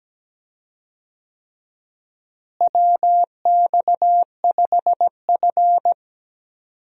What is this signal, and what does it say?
Morse code, heard as a keyed tone, about 17 wpm, 700 Hz. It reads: WX5F
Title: USA Call Signs - 4 Characters - Repeated 3x - 17wpm